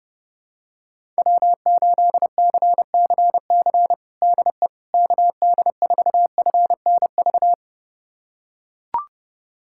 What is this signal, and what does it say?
Morse code, heard as a keyed tone, about 30 wpm, 700 Hz. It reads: W8CCC DE KD4FNV
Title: USA Callsign DE Callsign 30wpm